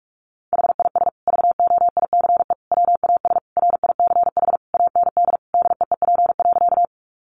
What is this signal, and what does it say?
Morse code, heard as a keyed tone, about 44 wpm, 700 Hz. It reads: HIS VOICE WAS RICH AND DEEP.
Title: Sentences from Top 1100 Words 45wpm